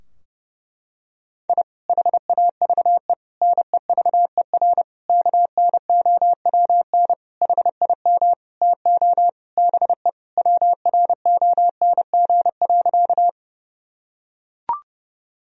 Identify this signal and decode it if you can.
Morse code, heard as a keyed tone, about 32 wpm, 700 Hz. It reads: I HAVE NEVER KNOWN HIM TO BE WRONG.
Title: Sentences from Top 1000 Words - Review Part 3 - 30wpm